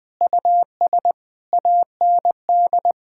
Morse code, heard as a keyed tone, about 21 wpm, 700 Hz. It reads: US AND
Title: Sentences with 6-Word Phrases - Spoken Beginning - 20wpm